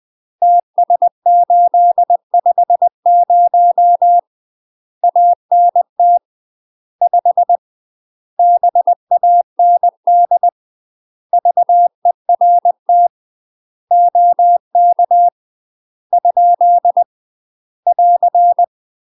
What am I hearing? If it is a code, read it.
Morse code, heard as a keyed tone, about 20 wpm, 700 Hz. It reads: TS850 ANT 5 BAND VERT OK ? <AR>
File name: UK MASTER QSOs TRAINING / 20WPM / UK MASTER QSOs 0066